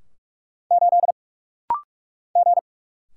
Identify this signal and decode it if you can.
Morse code, heard as a keyed tone, about 45 wpm, 700 Hz. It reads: 8 G
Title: Single Letter-Number Alt - Rapid-Fire 45wpm